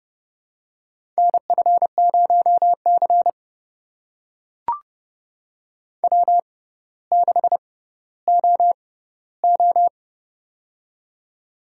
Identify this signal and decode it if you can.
Morse code, heard as a keyed tone, about 30 wpm, spaced as for 15 wpm, 700 Hz. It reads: NF0C W6OO
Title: USA Call Signs - 4 Characters - 6x Character Spacing - 30wpm